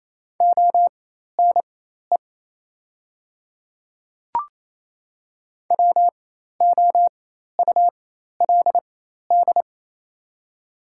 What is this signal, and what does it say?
Morse code, heard as a keyed tone, about 28 wpm, spaced as for 13 wpm, 690 Hz. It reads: ONE WOULD
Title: 4x Character Spacing - Top 200 Words - 28wpm